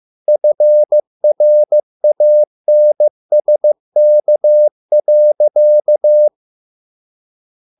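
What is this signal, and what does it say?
Morse code, heard as a keyed tone, about 15 wpm, 600 Hz. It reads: FRANSK.
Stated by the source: Morse code at a slow 15 wpm from CW med Gnister.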